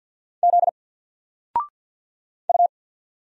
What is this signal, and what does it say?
Morse code, heard as a keyed tone, about 50 wpm, 700 Hz. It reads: Z U
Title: Single Character (Letter:Number:Punctuation:Prosign) 50wpm